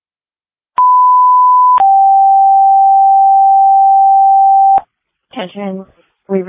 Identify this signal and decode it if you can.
Two-tone page, heard as a one-second tone, then a three-second tone, about 1000 then 780 Hz, with speech. It is heard from 0.8 to 4.8 s.